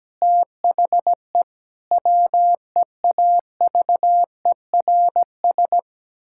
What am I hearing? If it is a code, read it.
Morse code, heard as a keyed tone, about 17 wpm, 700 Hz. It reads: THE WEAVERS
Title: Pangram Challenge v2 - 17wpm